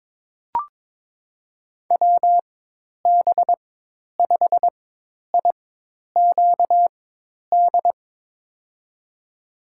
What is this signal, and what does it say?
Morse code, heard as a keyed tone, about 22 wpm, spaced as for 10 wpm, 700 Hz. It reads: WB5IQD
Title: USA Call Signs - 6 Characters - 4x Character Spacing - 22wpm